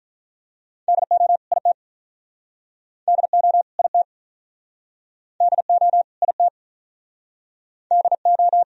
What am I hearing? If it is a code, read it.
Morse code, heard as a keyed tone, about 51 wpm, 700 Hz. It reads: DO IT DO IT DO IT DO